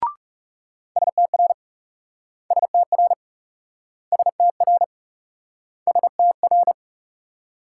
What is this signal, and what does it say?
Morse code, heard as a keyed tone, about 40 wpm, 700 Hz. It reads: STR STR STR STR